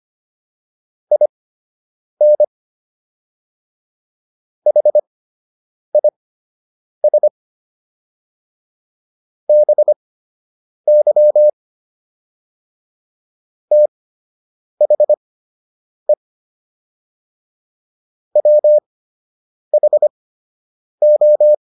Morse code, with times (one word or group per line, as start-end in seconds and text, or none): 1.1-2.4 IN
4.7-7.3 HIS
9.5-11.5 BY
13.7-16.1 THE
18.4-21.5 WHO